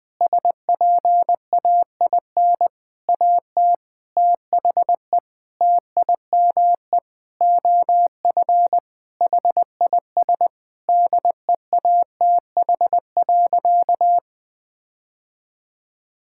Morse code, heard as a keyed tone, about 20 wpm, 700 Hz. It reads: SPAIN AT THE TIME OF HIS DEATH.
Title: Sentences with 6-Word Phrases - Spoken Beginning - 20wpm